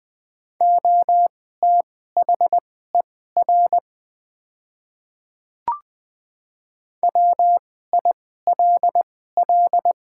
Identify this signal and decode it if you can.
Morse code, heard as a keyed tone, about 20 wpm, spaced as for 14 wpm, 700 Hz. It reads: OTHER WILL
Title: 2x Character Spacing - Top 100 Words - 20wpm